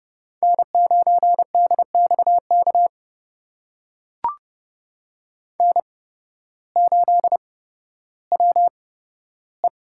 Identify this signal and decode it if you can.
Morse code, heard as a keyed tone, about 30 wpm, spaced as for 14 wpm, 700 Hz. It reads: N9DXK N8WE
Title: USA Call Signs - 5 Characters - 8x Character Spacing - 30wpm